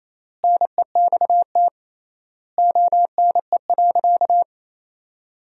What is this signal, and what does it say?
Morse code, heard as a keyed tone, about 28 wpm, 700 Hz. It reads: NEXT ONE.
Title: 3x Word-Spacing - Sentences from Top 400 Words - 28wpm